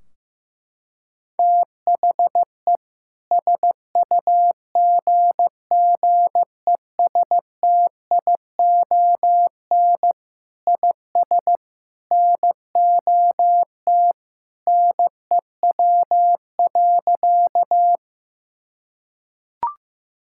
Morse code, heard as a keyed tone, about 15 wpm, 700 Hz. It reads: THE SUGGESTION IS NOT NEW.
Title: Sentences from Top 1600 Words 15wpm